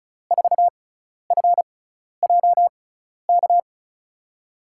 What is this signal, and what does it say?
Morse code, heard as a keyed tone, about 36 wpm, spaced as for 12 wpm, 700 Hz. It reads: 4FJK
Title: USA Call Signs - 6x Character Spacing - 35wpm